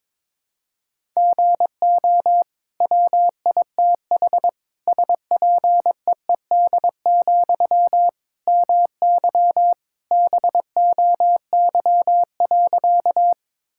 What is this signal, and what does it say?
Morse code, heard as a keyed tone, about 22 wpm, 700 Hz. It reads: GO WITH SPEED, MY BOY.
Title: Sentences from Top 1000 Words - Review Part 5 - 22wpm